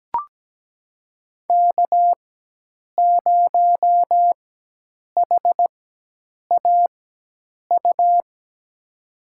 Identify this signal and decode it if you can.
Morse code, heard as a keyed tone, about 17 wpm, spaced as for 8 wpm, 700 Hz. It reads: K0HAU
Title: USA Call Signs - 5 Characters - 4x Character Spacing - 17wpm